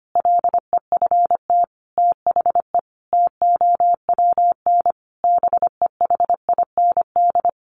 Morse code, heard as a keyed tone, about 25 wpm, 700 Hz. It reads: LEFT THE TOWN BEHIND
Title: Sentences from Top 1300 Words 25wpm